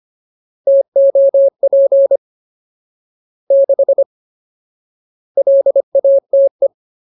Morse code, heard as a keyed tone, about 25 wpm, 550 Hz. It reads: TOP 6 LATE
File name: Ultimate Word List / 25WPM Ultimate